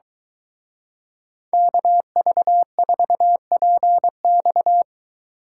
Morse code, heard as a keyed tone, about 23 wpm, 700 Hz. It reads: KV4PX